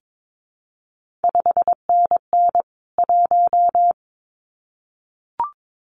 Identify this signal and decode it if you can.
Morse code, heard as a keyed tone, about 22 wpm, 700 Hz. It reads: 5NN 1
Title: CQ WW DX Contest Exchange Encore - 22wpm